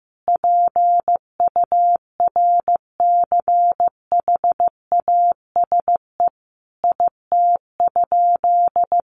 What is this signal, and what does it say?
Morse code, heard as a keyed tone, about 15 wpm, 700 Hz. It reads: PURCHASE IT?
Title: Sentences from Top 800 Words 15wpm